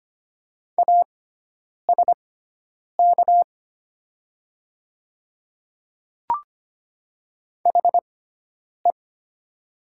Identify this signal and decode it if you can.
Morse code, heard as a keyed tone, about 25 wpm, spaced as for 9 wpm, 700 Hz. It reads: ASK HE
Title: 6x Character Spacing - Top 300 Words - 25wpm